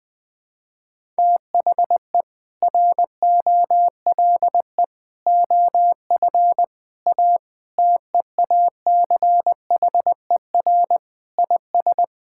Morse code, heard as a keyed tone, about 21 wpm, 700 Hz. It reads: THE ROLE OF A TEACHER IS
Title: Sentences with 3-Word Phrases - 20wpm